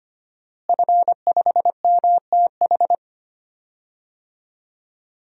Morse code, heard as a keyed tone, about 25 wpm, 700 Hz. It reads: F5MTH
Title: DXCC-100 Call Signs - 5 Characters - Repeated 2x - 25wpm